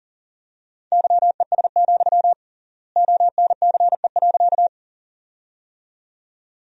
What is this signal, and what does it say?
Morse code, heard as a keyed tone, about 40 wpm, 700 Hz. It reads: YES, ONCE.
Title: 3x Word-Spacing - Sentences from Top 500 Words - 40wpm